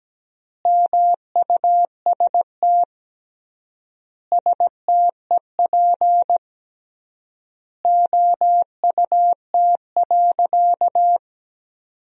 Morse code, heard as a keyed tone, about 17 wpm, 700 Hz. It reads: MUST STEP OUT.
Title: 3x Word-Spacing - Sentences from Top 400 Words - 17wpm